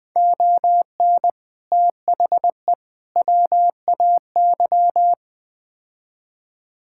Morse code, heard as a keyed tone, about 20 wpm, 700 Hz. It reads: ON THE WAY
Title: Build A Phrase - 5-Word Phrases - 20wpm